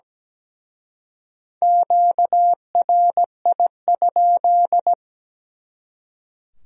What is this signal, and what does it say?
Morse code, heard as a keyed tone, about 20 wpm, 700 Hz. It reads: QRI?
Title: Speed-Racing Q-Codes - 17wpm